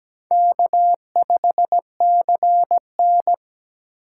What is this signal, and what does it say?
Morse code, heard as a keyed tone, about 20 wpm, 700 Hz. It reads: K5CN
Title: Speed Racing - USA 1x2 Call Signs 17wpm